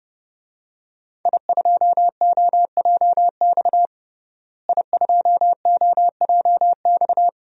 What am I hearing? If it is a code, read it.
Morse code, heard as a keyed tone, about 32 wpm, 700 Hz. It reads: I2OJX I2OJX
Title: DXCC-100 Call Signs - 5 Characters - Repeated 3x - 30wpm